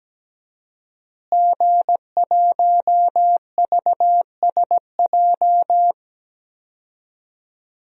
Morse code, heard as a keyed tone, about 17 wpm, 700 Hz. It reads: G1VSJ
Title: DXCC-100 Call Signs - 5 Characters - Repeated 2x - 17wpm